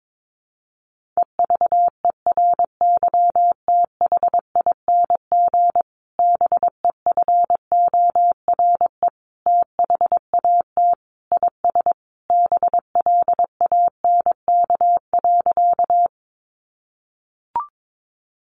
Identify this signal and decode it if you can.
Morse code, heard as a keyed tone, about 22 wpm, 700 Hz. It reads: EVERYTHING BEFORE THAT IS BLANK.
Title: Sentences from Top 1500 Words 22wpm